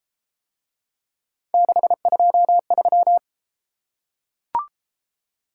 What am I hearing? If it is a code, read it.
Morse code, heard as a keyed tone, about 28 wpm, 700 Hz. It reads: 623